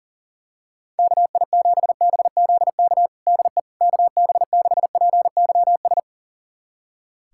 Morse code, heard as a keyed tone, about 37 wpm, 700 Hz. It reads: KI7BZK DE KB6PYS
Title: USA Callsign DE Callsign - 6 Characters - 40wpm